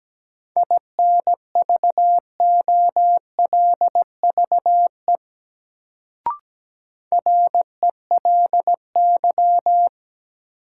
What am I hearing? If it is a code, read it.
Morse code, heard as a keyed tone, about 17 wpm, 700 Hz. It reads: INVOLVE RELY
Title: Sentences from Top 1200 Words - New Word Drill - 17wpm